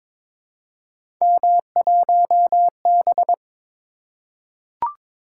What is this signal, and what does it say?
Morse code, heard as a keyed tone, about 22 wpm, 700 Hz. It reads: M1B
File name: practice / 22wpm / Three Letter-Number Combinations - 22wpm